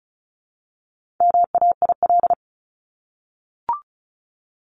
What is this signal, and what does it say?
Morse code, heard as a keyed tone, about 34 wpm, 700 Hz. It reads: MAIL
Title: Top 1751-2000 Words 35wpm